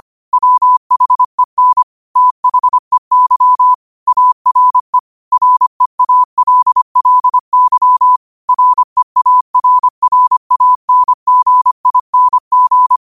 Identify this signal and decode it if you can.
Morse code, heard as a keyed tone, about 25 wpm, 1000 Hz. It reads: WHEN THEY ARE REALLY REARRANGING